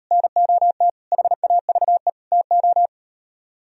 Morse code, heard as a keyed tone, about 38 wpm, 700 Hz. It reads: NOT HAVE TO